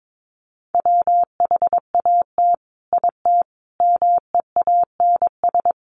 Morse code, heard as a keyed tone, about 22 wpm, 700 Hz. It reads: WHAT IT MEANS
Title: Sets of 5 Words 22wpm